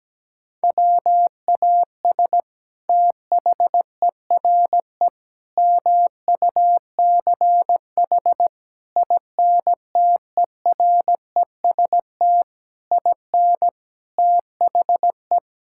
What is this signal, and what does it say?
Morse code, heard as a keyed tone, about 17 wpm, 700 Hz. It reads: WAS THERE MUCH INTEREST IN THE
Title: Sentences from Top 1000 Words - Review Part 5 - 17wpm